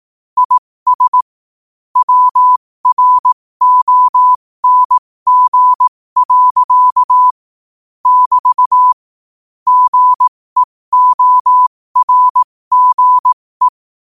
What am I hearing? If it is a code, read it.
Morse code, heard as a keyed tone, about 18 wpm, spaced as for 15 wpm, 1000 Hz. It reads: IS WRONG. <BT> GEORGE
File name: qotdfiles / 15wpm